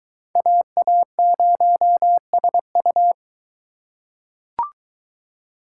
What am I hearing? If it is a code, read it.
Morse code, heard as a keyed tone, about 23 wpm, 700 Hz. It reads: AA0SU